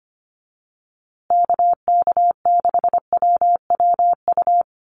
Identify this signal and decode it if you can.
Morse code, heard as a keyed tone, about 25 wpm, 700 Hz. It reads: KK6WWU
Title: USA Call Signs Repeated 3x 25wpm